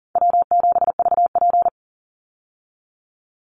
Morse code, heard as a keyed tone, about 40 wpm, 700 Hz. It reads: W7VP
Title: Call Signs Repeated 2x - 40wpm